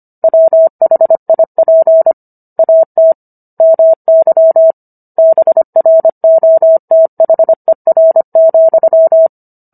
Morse code, heard as a keyed tone, about 25 wpm, 650 Hz. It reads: WHIP AT MY BROTHER,